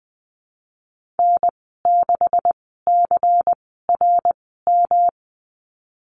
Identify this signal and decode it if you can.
Morse code, heard as a keyed tone, about 20 wpm, spaced as for 15 wpm, 700 Hz. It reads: N6CRM
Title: 128 Lesson 21g - Number 6 - Call Signs 20wpm